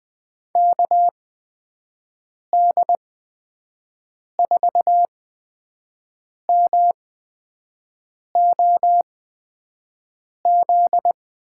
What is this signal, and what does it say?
Morse code, heard as a keyed tone, about 20 wpm, spaced as for 5 wpm, 700 Hz. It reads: KD4MOZ
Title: USA Call Signs - 6 Characters - 8x Character Spacing - 20wpm